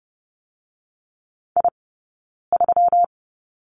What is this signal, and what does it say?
Morse code, heard as a keyed tone, about 30 wpm, 700 Hz. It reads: I 3
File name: Lesson 20d - Number 3 - Two Characters 30wpm